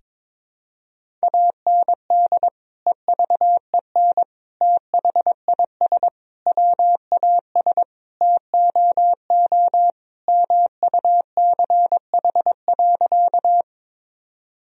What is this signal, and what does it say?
Morse code, heard as a keyed tone, about 22 wpm, 700 Hz. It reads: AND EVEN THIS WAS TOO MUCH.
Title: Sentences from Top 1000 Words - Review Part 1 - 22wpm